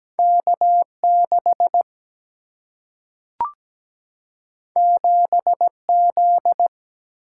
Morse code, heard as a keyed tone, about 17 wpm, 700 Hz. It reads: K6 7Z
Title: Single-Letter Single-Number 17wpm